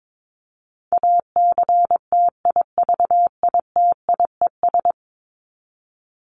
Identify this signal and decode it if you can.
Morse code, heard as a keyed tone, about 22 wpm, 700 Hz. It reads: ACTIVITIES
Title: ES:S Suffix - Top 100 Words with Base Words 22wpm